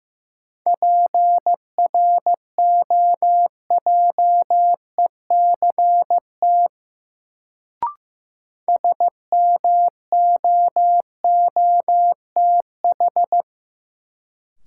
Morse code, heard as a keyed tone, about 15 wpm, 700 Hz. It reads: PROJECT SMOOTH
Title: Sentences from Top 1400 Words - New Word Drill - 15wpm